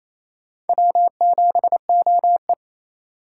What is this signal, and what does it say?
Morse code, heard as a keyed tone, about 34 wpm, 700 Hz. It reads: W7OE
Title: Speed Racing - USA 1x2 Call Signs 28wpm